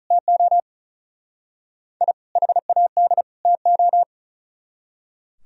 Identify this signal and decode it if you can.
Morse code, heard as a keyed tone, about 44 wpm, 700 Hz. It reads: TO I HAD TO